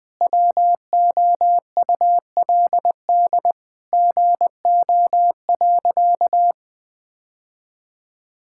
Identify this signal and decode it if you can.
Morse code, heard as a keyed tone, about 20 wpm, 700 Hz. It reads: WOULD GO.
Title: Sentences with 2-Word Phrases - Spoken Beginning - 20wpm